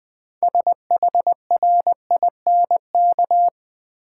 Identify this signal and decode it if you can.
Morse code, heard as a keyed tone, about 21 wpm, 700 Hz. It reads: SHRINK
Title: 6 Letter Words - First 3 Letters Given - 20wpm